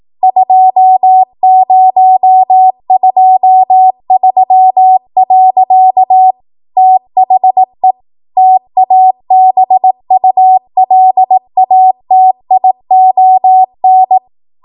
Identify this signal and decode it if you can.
Morse code, heard as a keyed tone, about 18 wpm, 750 Hz. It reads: 2023. THE TABULATION